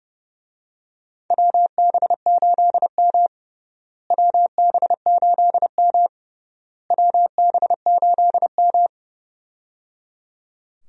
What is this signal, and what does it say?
Morse code, heard as a keyed tone, about 30 wpm, 700 Hz. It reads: WB8M WB8M WB8M
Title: USA Call Signs - 4 Characters - Repeated 3x - 30wpm